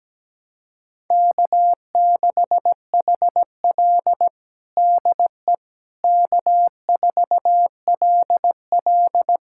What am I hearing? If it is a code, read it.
Morse code, heard as a keyed tone, about 17 wpm, 700 Hz. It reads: K6HL DE K4LL
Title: USA Callsign DE Callsign - 4 Characters - 17wpm